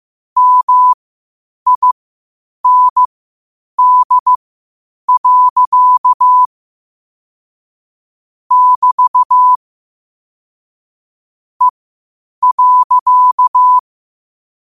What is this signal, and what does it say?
Morse code, heard as a keyed tone, about 15 wpm, spaced as for 8 wpm, 1000 Hz. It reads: MIND. <BT> E.
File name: qotdfiles / 8wpm